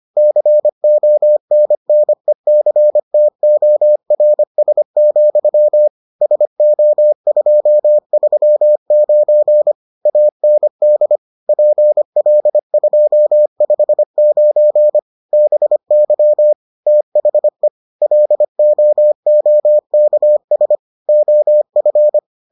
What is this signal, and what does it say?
Morse code, heard as a keyed tone, about 25 wpm, 600 Hz. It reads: CONNECTORS, SO239 AND PL259 BY THE LOOKS OF